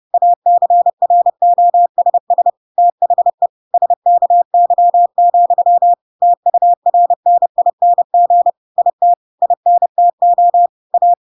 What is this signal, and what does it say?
Morse code, heard as a keyed tone, about 30 wpm, 700 Hz. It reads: ACROSS THE SKY, TURNING IT INTO A
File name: Pinocchio / Pinocchio ch6 30 30 ik1hge